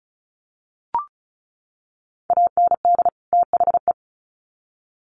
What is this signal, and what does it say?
Morse code, heard as a keyed tone, about 34 wpm, 700 Hz. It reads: AND THE
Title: Sets of 2 Words 35wpm